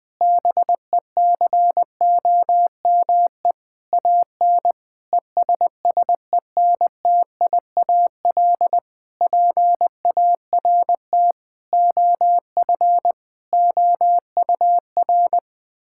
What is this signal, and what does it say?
Morse code, heard as a keyed tone, about 21 wpm, 700 Hz. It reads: BECOME AN ESSENTIAL PART OF OUR
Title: Difficult Sentences with Hints - 20wpm